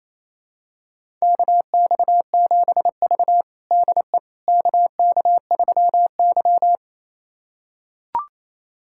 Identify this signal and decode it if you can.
Morse code, heard as a keyed tone, about 27 wpm, 700 Hz. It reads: KX7V DE KK3Y
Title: USA Callsign DE Callsign - 4 Characters - 28wpm